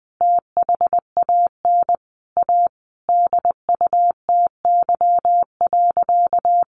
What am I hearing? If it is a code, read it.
Morse code, heard as a keyed tone, about 20 wpm, 700 Hz. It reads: THAN A DUTY.
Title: Sentences from Top 1500 Words 20wpm